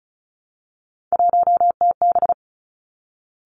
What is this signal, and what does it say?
Morse code, heard as a keyed tone, about 36 wpm, 700 Hz. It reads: 1TB
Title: Three Challenging Letter-Number Combinations 35wpm